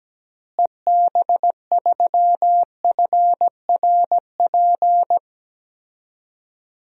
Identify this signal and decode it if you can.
Morse code, heard as a keyed tone, about 17 wpm, 700 Hz. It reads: EB3FRP
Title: DXCC-100 Call Signs - 6 Characters - Repeated 2x - 17wpm